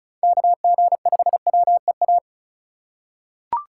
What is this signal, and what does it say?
Morse code, heard as a keyed tone, about 36 wpm, 700 Hz. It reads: KG5WEA
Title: USA Call Signs - 6 Characters - 4x Character Spacing - 35wpm